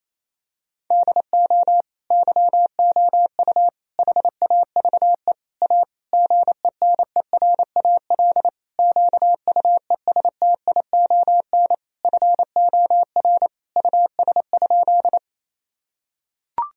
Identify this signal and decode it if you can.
Morse code, heard as a keyed tone, about 28 wpm, 700 Hz. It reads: DO YOU HAVE A GENERAL QUESTION FOR US?
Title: Sentences from Top 1000 Words - Review Part 3 - 28wpm